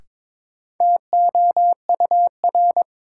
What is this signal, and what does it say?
Morse code, heard as a keyed tone, about 22 wpm, 700 Hz. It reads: TOUR
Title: Sentences from Top 1400 Words - New Word Drill Alt - 22wpm